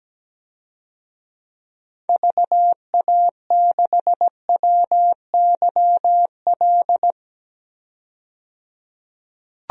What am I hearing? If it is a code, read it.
Morse code, heard as a keyed tone, about 17 wpm, 700 Hz. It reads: VA6WYL
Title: DXCC-100 Call Signs - 6 Characters - 17wpm